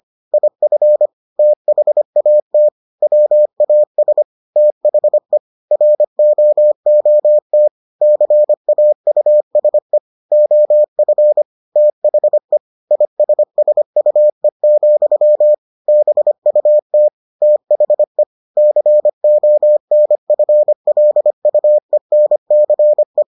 Morse code, heard as a keyed tone, about 25 wpm, 600 Hz. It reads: IF THAT WAS THE ROOT CAUSE OF THE ISSUE, BUT THE CONFLUENCE